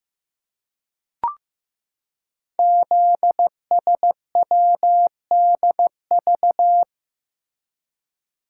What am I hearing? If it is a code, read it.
Morse code, heard as a keyed tone, about 15 wpm, 700 Hz. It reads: ZSWDV
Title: Code Groups - 5 Letters - 15wpm